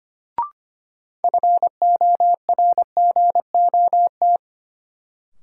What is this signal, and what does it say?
Morse code, heard as a keyed tone, about 25 wpm, 700 Hz. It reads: FORGOT
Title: Sentences from Top 1400 Words - New Word Drill - 25wpm